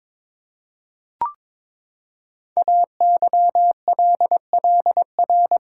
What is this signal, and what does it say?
Morse code, heard as a keyed tone, about 22 wpm, 700 Hz. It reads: AYLLR
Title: Code Groups - 5 Letters - 22wpm